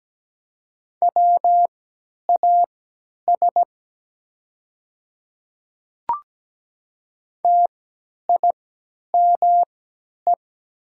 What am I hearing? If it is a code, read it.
Morse code, heard as a keyed tone, about 17 wpm, spaced as for 10 wpm, 700 Hz. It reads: WAS TIME
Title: 3x Character Spacing - Top 100 Words - 17wpm